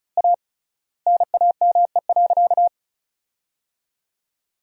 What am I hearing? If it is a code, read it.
Morse code, heard as a keyed tone, about 34 wpm, 700 Hz. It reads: A NAME.
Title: 3x Word-Spacing - Sentences from Top 500 Words - 35wpm